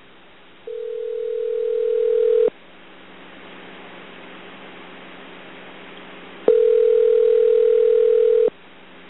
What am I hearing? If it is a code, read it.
Ringback tone (North America), 440 + 480 Hz.